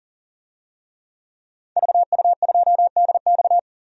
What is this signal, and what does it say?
Morse code, heard as a keyed tone, about 40 wpm, 700 Hz. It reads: VU2DX
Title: DXCC-100 Call Signs - 5 Characters - 40wpm